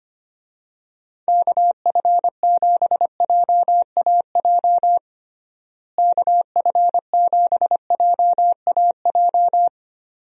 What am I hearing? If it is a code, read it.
Morse code, heard as a keyed tone, about 25 wpm, 700 Hz. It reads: KF7JAJ KF7JAJ
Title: USA Call Signs - 6 Characters - Repeated 2x - 25wpm